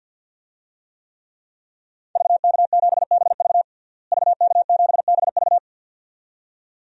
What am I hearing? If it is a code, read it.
Morse code, heard as a keyed tone, about 48 wpm, 700 Hz. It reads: VK7BV VK7BV
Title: DXCC-100 Call Signs - 5 Characters - Repeated 2x - 50wpm